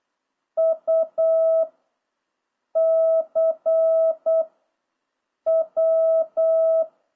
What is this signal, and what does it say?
Morse code, heard as a keyed tone, about 8 wpm, 660 Hz. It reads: U C W